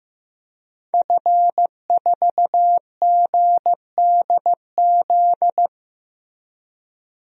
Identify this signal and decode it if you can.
Morse code, heard as a keyed tone, about 15 wpm, 700 Hz. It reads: F4GDZ
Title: DXCC-100 Call Signs - 5 Characters - 15wpm